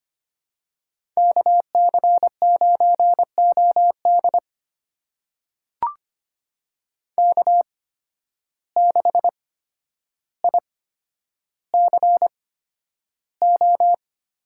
Morse code, heard as a keyed tone, about 25 wpm, spaced as for 11 wpm, 700 Hz. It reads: KC9OD K6ICO
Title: USA Call Signs - 5 Characters - 8x Character Spacing - 25wpm